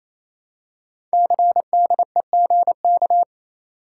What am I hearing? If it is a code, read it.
Morse code, heard as a keyed tone, about 27 wpm, 700 Hz. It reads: CDEGK
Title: Code Groups - 5 Letters - 28wpm